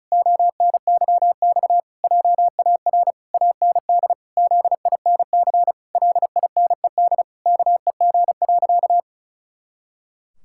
Morse code, heard as a keyed tone, about 34 wpm, 700 Hz. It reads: ONYX JAR AND ZINC LINED KEG.
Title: Pangram Challenge v2 - 35wpm